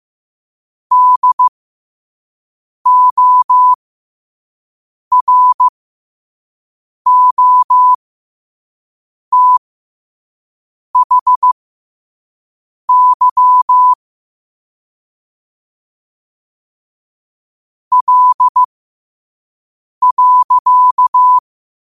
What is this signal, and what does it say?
Morse code, heard as a keyed tone, about 15 wpm, spaced as for 5 wpm, 1000 Hz. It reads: DOROTHY L.
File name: qotdfiles / 5wpm